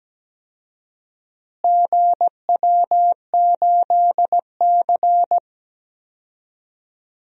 Morse code, heard as a keyed tone, about 17 wpm, 700 Hz. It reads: GW8C
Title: DXCC-100 Call Signs - 4 Characters - Repeated 2x - 17wpm